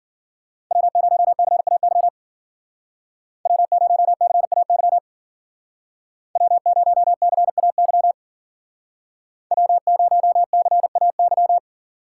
Morse code, heard as a keyed tone, about 50 wpm, 700 Hz. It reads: W0CAY W0CAY W0CAY W0CAY